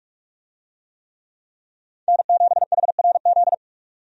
Morse code, heard as a keyed tone, about 42 wpm, 700 Hz. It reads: N7HRZ
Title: USA Call Signs - 5 Characters - 45wpm